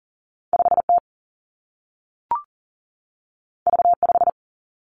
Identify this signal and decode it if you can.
Morse code, heard as a keyed tone, about 42 wpm, 700 Hz. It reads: 5 T V 5
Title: Two Challenging Letter-Number Combinations 40wpm